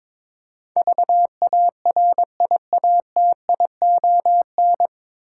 Morse code, heard as a keyed tone, about 22 wpm, 700 Hz. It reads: VARIATION
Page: Morse Code Ninja practice 22wpm